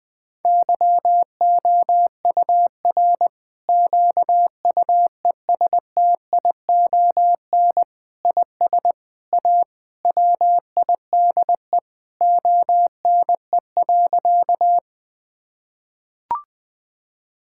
Morse code, heard as a keyed tone, about 20 wpm, 700 Hz. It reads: YOUR QUESTION IS A WIDE ONE.
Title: Sentences from Top 1000 Words - Review Part 6 - 20wpm